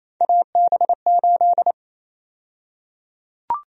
Morse code, heard as a keyed tone, about 27 wpm, 700 Hz. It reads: AB8
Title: USA Call Sign Prefixes - 28wpm